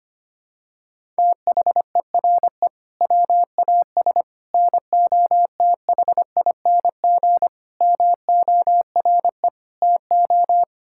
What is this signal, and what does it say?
Morse code, heard as a keyed tone, about 25 wpm, 700 Hz. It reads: THERE WAS NOTHING MORE TO
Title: Sentences from Top 1000 Words - Review Part 5 - 25wpm